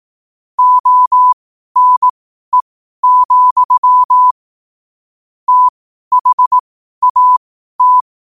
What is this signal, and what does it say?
Morse code, heard as a keyed tone, about 18 wpm, spaced as for 12 wpm, 1000 Hz. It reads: ONE, THAT
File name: qotdfiles / 12wpm